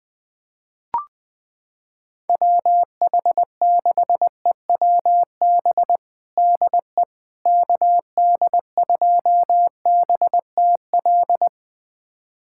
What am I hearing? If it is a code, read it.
Morse code, heard as a keyed tone, about 20 wpm, 700 Hz. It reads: WH6EWB DE KD2BTL
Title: USA Callsign DE Callsign - 6 Characters - 20wpm